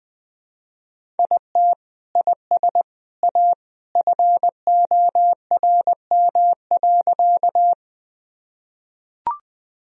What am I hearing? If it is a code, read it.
Morse code, heard as a keyed tone, about 19 wpm, 700 Hz. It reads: IT IS A FORM.
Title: Sentences from Top 1000 Words - Review Part 1 - 20wpm